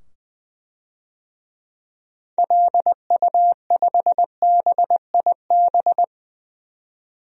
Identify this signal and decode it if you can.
Morse code, heard as a keyed tone, about 20 wpm, 700 Hz. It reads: LU5BIB
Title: DXCC-100 Call Signs - 6 Characters - 20wpm